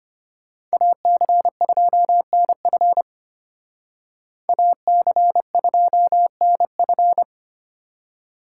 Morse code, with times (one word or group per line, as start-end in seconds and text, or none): 0.7-3.0 AC2NF
4.5-7.2 AC2NF